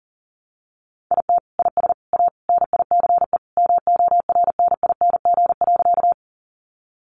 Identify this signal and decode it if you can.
Morse code, heard as a keyed tone, about 37 wpm, 700 Hz. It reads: IT IS A NICE MORNING.
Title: Sentences from Top 1100 Words 40wpm